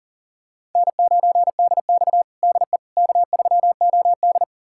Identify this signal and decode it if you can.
Morse code, heard as a keyed tone, about 40 wpm, 700 Hz. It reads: N9DX DE K3OD
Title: USA Callsign DE Callsign - 4 Characters - 40wpm